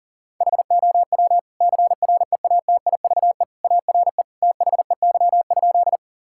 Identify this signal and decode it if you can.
Morse code, heard as a keyed tone, about 40 wpm, 700 Hz. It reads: HOW CREATIVE ARE THEY?
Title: Sentences from Top 1000 Words - Review Part 5 - 40wpm